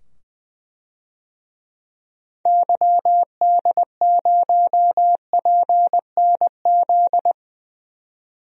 Morse code, heard as a keyed tone, about 21 wpm, 700 Hz. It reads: YD0PNZ
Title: DXCC-100 Call Signs - 6 Characters - 20wpm